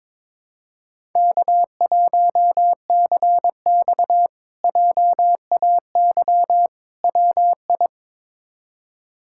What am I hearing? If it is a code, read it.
Morse code, heard as a keyed tone, about 22 wpm, 700 Hz. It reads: K1CX JAY WI
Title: K1USN SST QSO Practice - 22wpm